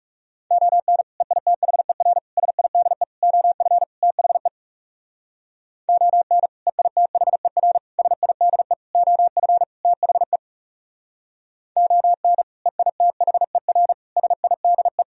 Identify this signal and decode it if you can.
Morse code, heard as a keyed tone, about 36 wpm, 700 Hz. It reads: ON EITHER SIDE OF THE ON EITHER SIDE OF THE ON EITHER SIDE
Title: Speed-Racing - Sets of 5 Words 30wpm